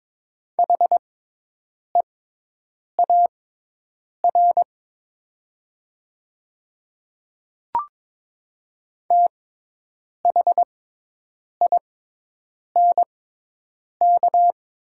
Morse code, heard as a keyed tone, about 22 wpm, spaced as for 8 wpm, 700 Hz. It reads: HEAR THINK
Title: 6x Character Spacing - Top 300 Words - 22wpm